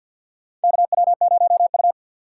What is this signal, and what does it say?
Morse code, heard as a keyed tone, about 47 wpm, 700 Hz. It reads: KW0U
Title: USA Call Signs - 6x Character Spacing - 50wpm